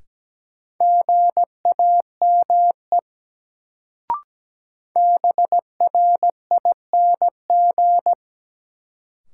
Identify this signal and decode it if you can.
Morse code, heard as a keyed tone, about 17 wpm, 700 Hz. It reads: GAME BRING
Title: Sentences from Top 500 Words - New Word Drill - 17wpm